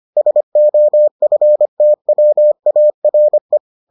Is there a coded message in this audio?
Yes: Morse code from 0.2 to 3.6 s, heard as a keyed tone, about 25 wpm, 600 Hz. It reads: SOFTWARE